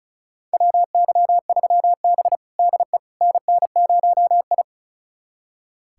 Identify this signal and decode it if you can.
Morse code, heard as a keyed tone, about 36 wpm, 700 Hz. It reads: WY3B DE NN0I
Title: USA Callsign DE Callsign - 4 Characters - 35wpm